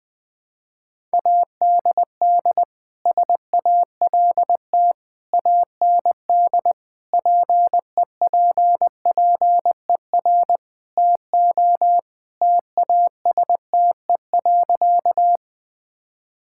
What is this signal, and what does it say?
Morse code, heard as a keyed tone, about 20 wpm, 700 Hz. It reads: ADD SALT AND PEPPER TO TASTE.
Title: Sentences with 5-Word Phrases - 20wpm